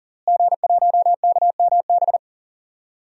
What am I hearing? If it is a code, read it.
Morse code, heard as a keyed tone, about 40 wpm, 700 Hz. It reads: G1KMB
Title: DXCC-100 Call Signs - 5 Characters - Repeated 2x - 40wpm